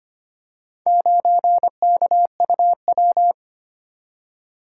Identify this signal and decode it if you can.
Morse code, heard as a keyed tone, about 26 wpm, 700 Hz. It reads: 9KUW
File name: practice / 25wpm / Four Letter-Number Combinations - 25wpm